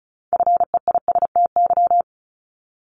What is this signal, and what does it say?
Morse code, heard as a keyed tone, about 34 wpm, 700 Hz. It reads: FEISTY
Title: Multi-letter Phonemes 35wpm